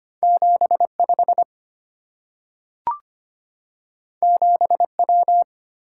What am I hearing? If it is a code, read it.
Morse code, heard as a keyed tone, about 25 wpm, 700 Hz. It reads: 75 7W
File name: Two Letter-Number Combinations - 25wpm